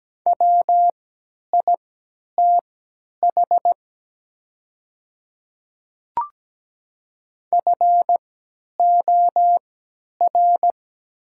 Morse code, heard as a keyed tone, about 17 wpm, spaced as for 10 wpm, 700 Hz. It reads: WITH FOR
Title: 3x Character Spacing - Top 100 Words - 17wpm